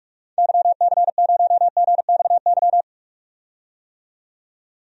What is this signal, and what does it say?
Morse code, heard as a keyed tone, about 41 wpm, 700 Hz. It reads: YC0GXY
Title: DXCC-100 Call Signs - 6 Characters - 45wpm